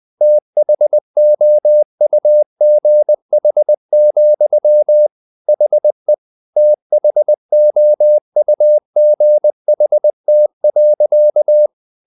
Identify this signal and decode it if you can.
Morse code, heard as a keyed tone, about 20 wpm, 600 Hz. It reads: THOUGH, HE THOUGHT.
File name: the old man and the sea / the old man and the sea part 1 20wpm